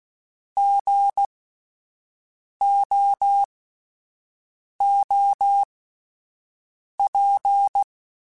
Morse code, heard as a keyed tone, about 16 wpm, spaced as for 5 wpm, 800 Hz. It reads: GOOP